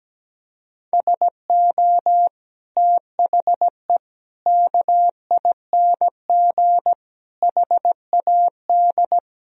Morse code, heard as a keyed tone, about 17 wpm, 700 Hz. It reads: SO THE KING HAD
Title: Sentences from Top 1000 Words - Review Part 1 - 17wpm